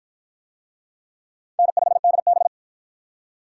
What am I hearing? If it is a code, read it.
Morse code, heard as a keyed tone, about 52 wpm, 700 Hz. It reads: N5DB